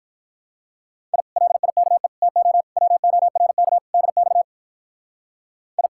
Morse code, heard as a keyed tone, about 43 wpm, 700 Hz. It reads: I LIKE TO WORK DX I
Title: Speed-Racing Common QSO Phrases - 35wpm